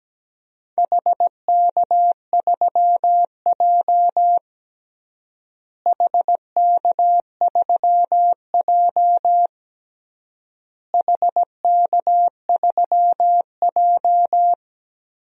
Morse code, heard as a keyed tone, about 17 wpm, 700 Hz. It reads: HK3J HK3J HK3J
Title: DXCC-100 Call Signs - 4 Characters - Repeated 3x - 17wpm